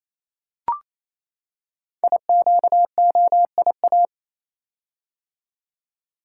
Morse code, heard as a keyed tone, about 29 wpm, 700 Hz. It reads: IQOIA
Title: Code Groups - 5 Letters - 28wpm